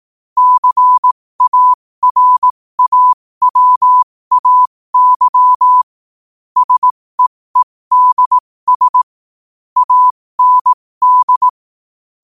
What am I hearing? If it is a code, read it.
Morse code, heard as a keyed tone, about 18 wpm, spaced as for 15 wpm, 1000 Hz. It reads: CARAWAY SEEDS AND